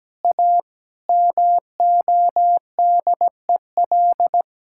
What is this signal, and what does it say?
Morse code, heard as a keyed tone, about 17 wpm, 700 Hz. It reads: A MODEL
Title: Sentences from Top 1000 Words - Review Part 5 - 17wpm